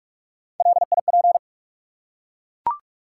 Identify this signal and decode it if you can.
Morse code, heard as a keyed tone, about 42 wpm, 700 Hz. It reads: LIP
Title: Three Letter Words - 45wpm